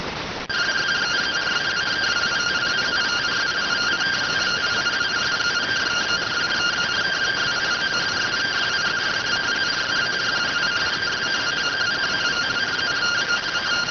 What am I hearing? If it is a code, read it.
Two-tone teleprinter signal, about 170 Hz shift.